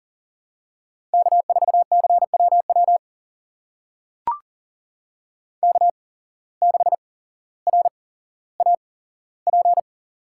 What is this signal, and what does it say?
Morse code, heard as a keyed tone, about 39 wpm, spaced as for 17 wpm, 700 Hz. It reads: K4CWW K6RAP
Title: USA Call Signs - 5 Characters - 8x Character Spacing - 40wpm